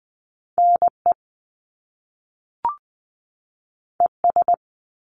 Morse code, heard as a keyed tone, about 20 wpm, 700 Hz. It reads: NE ES
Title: Top Two Letter Combinations in Words 20wpm